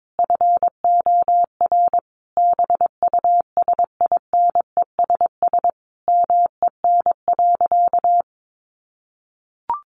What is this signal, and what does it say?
Morse code, heard as a keyed tone, about 22 wpm, 700 Hz. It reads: FOR BUSINESS MEN.
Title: Sentences from Top 1200 Words 22wpm